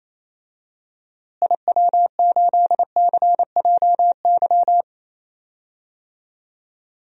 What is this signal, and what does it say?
Morse code, heard as a keyed tone, about 28 wpm, 700 Hz. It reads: IW8CJY
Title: DXCC-100 Call Signs - 6 Characters - Repeated 2x - 28wpm